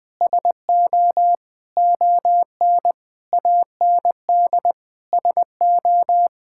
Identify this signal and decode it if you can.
Morse code, heard as a keyed tone, about 20 wpm, 700 Hz. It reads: SO ON AND SO
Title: Sentences with 6-Word Phrases - Spoken Beginning - 20wpm